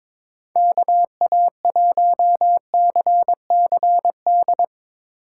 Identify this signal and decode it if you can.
Morse code, heard as a keyed tone, about 24 wpm, 700 Hz. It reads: KA1CCD
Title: Speed Racing - USA 2x3 Call Signs 22wpm